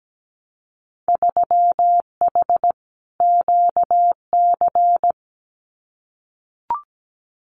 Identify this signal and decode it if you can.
Morse code, heard as a keyed tone, about 17 wpm, 700 Hz. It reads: 3H QC
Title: Winter Field Day Exchanges 17wpm